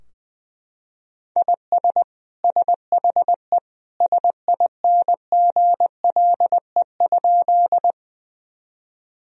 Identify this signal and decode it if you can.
Morse code, heard as a keyed tone, about 21 wpm, 700 Hz. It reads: IS SHE SINGLE?
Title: Sentences from Top 1000 Words - Review Part 4 - 20wpm